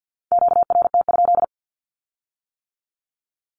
Morse code, heard as a keyed tone, about 50 wpm, 700 Hz. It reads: QRT?
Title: Q-Codes with Spoken Meaning - 50wpm